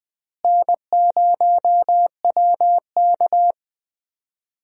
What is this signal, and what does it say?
Morse code, heard as a keyed tone, about 20 wpm, 700 Hz. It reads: N0WK
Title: USA Call Signs - 4 Characters - Repeated 2x - 20wpm